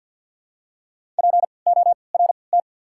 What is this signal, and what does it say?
Morse code, heard as a keyed tone, about 53 wpm, 700 Hz. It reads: P O R T
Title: 3x Character Spacing - Top 200 Words - 50wpm